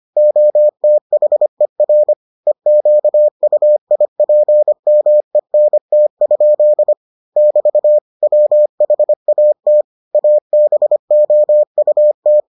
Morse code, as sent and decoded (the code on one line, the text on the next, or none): --- - .... .-. / . --.- ..- .. .--. -- . -. - ..--.. / -...- / .-- .... .- - / .- -... --- ..- -
OTHER EQUIPMENT? <BT> WHAT ABOUT